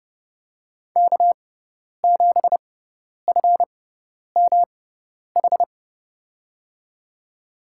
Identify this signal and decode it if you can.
Morse code, heard as a keyed tone, about 32 wpm, spaced as for 10 wpm, 700 Hz. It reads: K7FMH
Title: USA Call Signs - 5 Characters - 6x Character Spacing - 30wpm